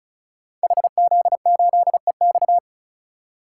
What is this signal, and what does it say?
Morse code, heard as a keyed tone, about 36 wpm, 700 Hz. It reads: HZ8EX
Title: Code Groups - 5 Letter:Numbers - 35wpm